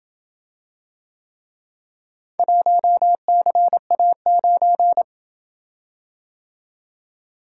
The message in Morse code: 1CA9

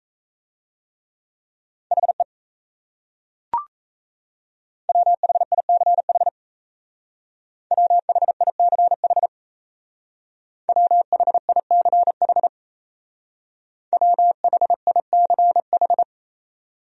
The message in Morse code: HE WHICH WHICH WHICH WHICH

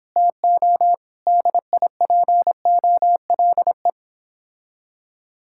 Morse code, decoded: TO DIPOLE

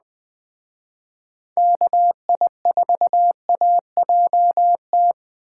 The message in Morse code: KI4AJT

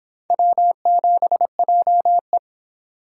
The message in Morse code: W7JE

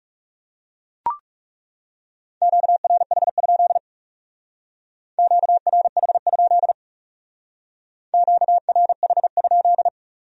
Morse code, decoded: QRH? QRH? QRH?

